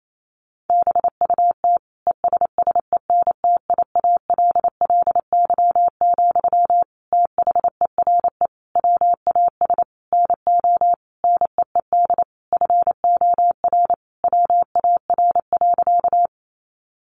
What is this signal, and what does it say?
Morse code, heard as a keyed tone, about 28 wpm, 700 Hz. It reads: BUT ESSENTIALLY, THERE WAS NO NEED FOR WAR.
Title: Sentences from Top 1300 Words 28wpm